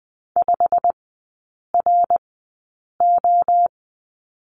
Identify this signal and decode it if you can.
Morse code, heard as a keyed tone, about 20 wpm, spaced as for 8 wpm, 700 Hz. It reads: 5RO
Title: 093 Lesson 16e - Letter c - Three Characters 20wpm